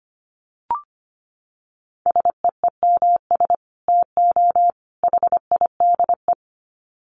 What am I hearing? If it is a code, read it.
Morse code, heard as a keyed tone, about 25 wpm, 700 Hz. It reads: SEEMS TO HIDE
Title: Dit Challenge 25wpm